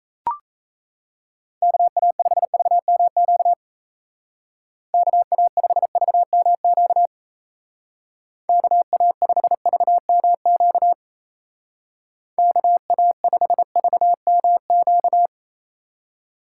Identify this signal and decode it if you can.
Morse code, heard as a keyed tone, about 35 wpm, 700 Hz. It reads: KA5VMQ KA5VMQ KA5VMQ KA5VMQ
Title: Speed Racing - USA 2x3 Call Signs 28wpm